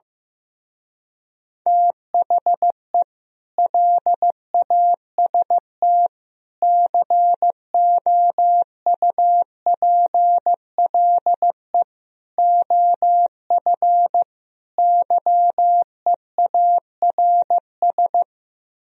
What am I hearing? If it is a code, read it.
Morse code, heard as a keyed tone, about 15 wpm, 700 Hz. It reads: THE LAST COUPLE OF YEARS
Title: Speed-Racing - Sets of 5 Words 15wpm